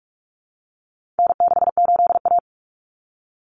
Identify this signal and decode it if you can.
Morse code, heard as a keyed tone, about 44 wpm, 700 Hz. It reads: N68A
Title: Common Cut Numbers - 4-Digit 45wpm